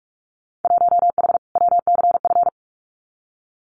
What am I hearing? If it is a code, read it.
Morse code, heard as a keyed tone, about 44 wpm, 700 Hz. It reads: 1H WCF
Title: Winter Field Day Exchanges 45wpm